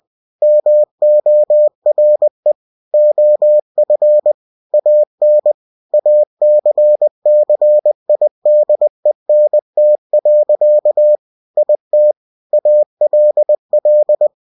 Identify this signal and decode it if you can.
Morse code, heard as a keyed tone, about 20 wpm, 600 Hz. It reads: MORE OF AN ACCIDENT. IT ALL
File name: cw-books / winnie-the-pooh / winnie-20wpm-page-0099